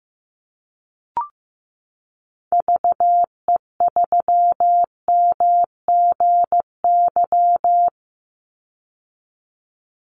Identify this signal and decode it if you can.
Morse code, heard as a keyed tone, about 15 wpm, 700 Hz. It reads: VE3MGY
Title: Call Signs Repeated 3x - 15wpm